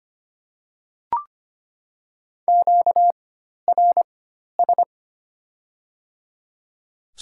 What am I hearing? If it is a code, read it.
Morse code, heard as a keyed tone, about 25 wpm, spaced as for 12 wpm, 700 Hz. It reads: QRS
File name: Top 90 QSO Elements - 4x Character Spacing - 25wpm